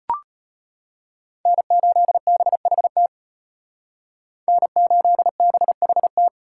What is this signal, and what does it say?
Morse code, heard as a keyed tone, about 32 wpm, 700 Hz. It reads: N8BHT N8BHT